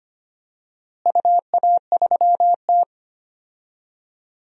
Morse code, heard as a keyed tone, about 25 wpm, 700 Hz. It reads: UA3T
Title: DXCC-100 Call Signs - 4 Characters - Repeated 3x - 25wpm